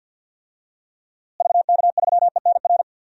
Morse code, heard as a keyed tone, about 51 wpm, 700 Hz. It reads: VK3ENR
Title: DXCC-100 Call Signs - 6 Characters - Repeated 3x - 50wpm